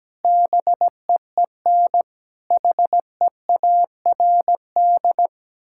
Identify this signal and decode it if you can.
Morse code, heard as a keyed tone, about 17 wpm, 700 Hz. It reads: BEEN HEARD
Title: Sentences from Top 1600 Words 17wpm